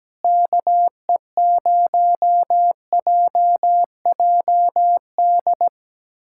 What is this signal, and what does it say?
Morse code, heard as a keyed tone, about 17 wpm, 700 Hz. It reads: KE0JJD
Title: USA Callsign DE Callsign 17wpm